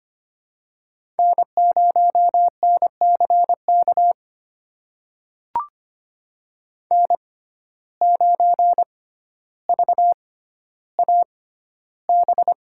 Morse code, heard as a keyed tone, about 25 wpm, spaced as for 13 wpm, 700 Hz. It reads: N0NCK N9VAB
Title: USA Call Signs - 5 Characters - 6x Character Spacing - 25wpm